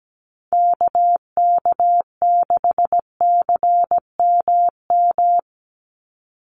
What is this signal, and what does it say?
Morse code, heard as a keyed tone, about 17 wpm, 700 Hz. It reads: KK6CMM
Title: USA Call Signs 17wpm